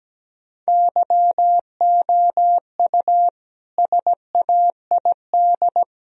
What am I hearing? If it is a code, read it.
Morse code, heard as a keyed tone, about 17 wpm, 700 Hz. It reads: YOU SAID
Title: Sentences from Top 1000 Words - Review Part 1 - 17wpm